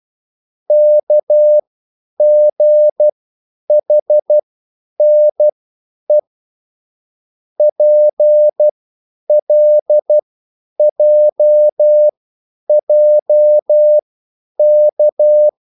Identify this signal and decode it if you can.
Morse code, heard as a keyed tone, about 12 wpm, spaced as for 9 wpm, 600 Hz. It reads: KGHNE PLJJK
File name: QSL / CW lesson